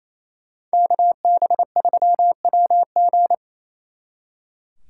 Morse code, heard as a keyed tone, about 36 wpm, 700 Hz. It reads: KB3WG